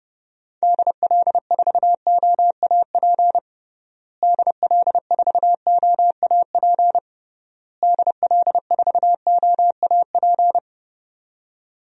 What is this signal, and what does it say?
Morse code, heard as a keyed tone, about 30 wpm, 700 Hz. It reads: DL4OAP DL4OAP DL4OAP
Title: USA Call Signs - 6 Characters - Repeated 3x - 30wpm